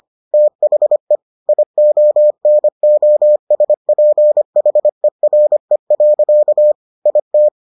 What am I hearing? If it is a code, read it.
Morse code, heard as a keyed tone, about 25 wpm, 600 Hz. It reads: THE IONOSPHERE. IT